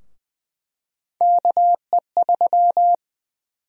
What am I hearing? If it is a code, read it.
Morse code, heard as a keyed tone, about 20 wpm, 700 Hz. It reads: KE3
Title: USA Call Sign Prefixes - Repeated 2x - 20wpm